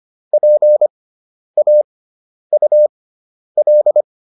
Morse code, heard as a keyed tone, about 25 wpm, spaced as for 10 wpm, 600 Hz. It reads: PAUL